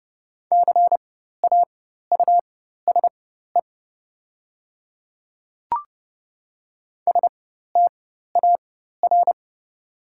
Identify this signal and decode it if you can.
Morse code, heard as a keyed tone, about 29 wpm, spaced as for 14 wpm, 700 Hz. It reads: CAUSE STAR
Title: 4x Character Spacing - Top 500 Words - 30wpm